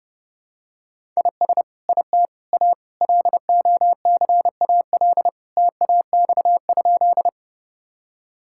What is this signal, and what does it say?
Morse code, heard as a keyed tone, about 30 wpm, 700 Hz. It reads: IS IT A LOCAL TAX?
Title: Sentences from Top 1000 Words - Review Part 5 - 30wpm